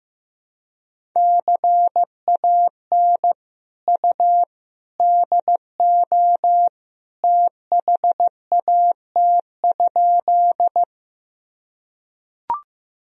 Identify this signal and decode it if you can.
Morse code, heard as a keyed tone, about 16 wpm, 700 Hz. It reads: CAN U DO THAT?
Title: Sentences from Top 100 Words and Abbreviations 15wpm